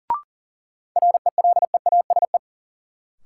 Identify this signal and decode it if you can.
Morse code, heard as a keyed tone, about 40 wpm, 700 Hz. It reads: RELEASE
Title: Sentences from Top 1100 Words - New Word Drill - 40wpm